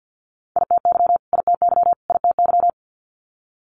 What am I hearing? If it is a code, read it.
Morse code, heard as a keyed tone, about 50 wpm, 700 Hz. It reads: ITY ITY ITY
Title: Multi-letter Phonemes Repeated 3x 50wpm